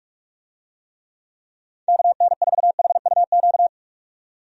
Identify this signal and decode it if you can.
Morse code, heard as a keyed tone, about 46 wpm, 700 Hz. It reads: KN4HUQ
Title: USA Call Signs - 6 Characters - 45wpm